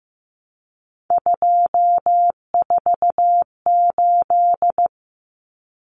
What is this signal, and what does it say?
Morse code, heard as a keyed tone, about 15 wpm, 700 Hz. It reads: 248
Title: Numbers - 3-Digit 15wpm